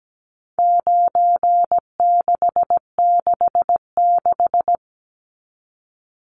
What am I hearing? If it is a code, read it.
Morse code, heard as a keyed tone, about 17 wpm, 700 Hz. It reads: 9666
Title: Numbers - 4-Digit 17wpm